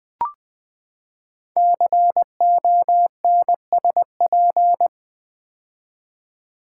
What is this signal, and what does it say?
Morse code, heard as a keyed tone, about 20 wpm, 700 Hz. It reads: CONSP
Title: Build A Word - 5 Letters At A Time - 10-Letter Words - 20wpm